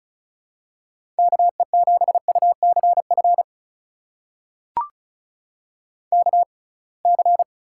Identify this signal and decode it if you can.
Morse code, heard as a keyed tone, about 36 wpm, spaced as for 27 wpm, 700 Hz. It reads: KE7UCF KC
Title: USA Call Signs - 6x Character Spacing - 35wpm